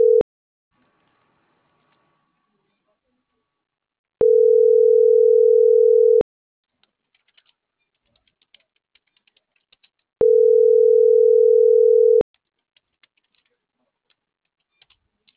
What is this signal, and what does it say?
Ringback tone (North America), 440 + 480 Hz.